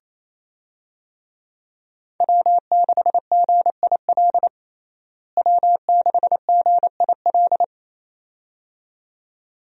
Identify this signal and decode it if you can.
Morse code, heard as a keyed tone, about 28 wpm, 700 Hz. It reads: W6GIL W6GIL
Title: USA Call Signs - 5 Characters - Repeated 2x - 28wpm